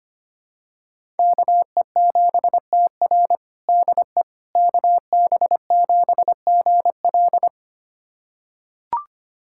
Morse code, heard as a keyed tone, about 25 wpm, 700 Hz. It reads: KE7TR DE KB7GL
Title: USA Callsign DE Callsign - 5 Characters - 25wpm